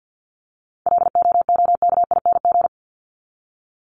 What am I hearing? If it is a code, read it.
Morse code, heard as a keyed tone, about 48 wpm, 700 Hz. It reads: LOOKING
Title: ING Suffix - Top 100 Words with Base Words 50wpm